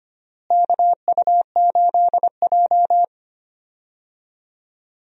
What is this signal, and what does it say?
Morse code, heard as a keyed tone, about 25 wpm, 700 Hz. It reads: KU8J
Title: USA Call Signs - 4 Characters - Repeated 2x - 25wpm